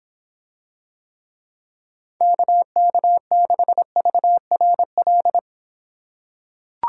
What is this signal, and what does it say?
Morse code, heard as a keyed tone, about 25 wpm, 700 Hz. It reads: KK6VRL